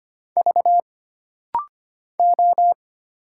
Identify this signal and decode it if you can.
Morse code, heard as a keyed tone, about 25 wpm, 700 Hz. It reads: V O
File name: Single Character (Letter:Number:Punctuation:Prosign) - Rapid-Fire - 25wpm